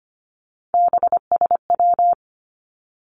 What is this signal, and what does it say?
Morse code, heard as a keyed tone, about 25 wpm, 700 Hz. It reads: BSW
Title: Three Challenging Letter-Number Combinations 25wpm